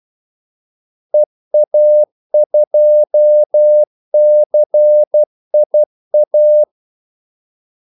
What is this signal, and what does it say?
Morse code, heard as a keyed tone, about 12 wpm, 600 Hz. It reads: EA2CIA
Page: 2 ekstra ordmellemrum mellem kaldesignalerne